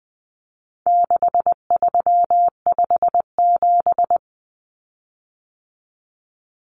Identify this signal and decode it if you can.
Morse code, heard as a keyed tone, about 20 wpm, 700 Hz. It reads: 6357
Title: Common Cut Numbers - 4-Digit 20wpm